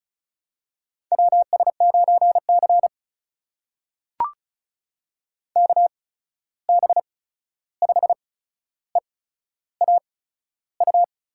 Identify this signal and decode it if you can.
Morse code, heard as a keyed tone, about 36 wpm, spaced as for 13 wpm, 700 Hz. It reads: WS9C KB5EAU
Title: USA Call Signs - 8x Character Spacing - 35wpm